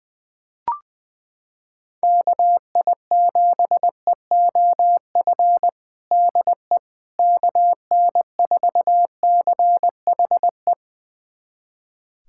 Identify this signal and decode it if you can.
Morse code, heard as a keyed tone, about 20 wpm, 700 Hz. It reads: KI7EOF DE KN4CHE
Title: USA Callsign DE Callsign - 6 Characters - 20wpm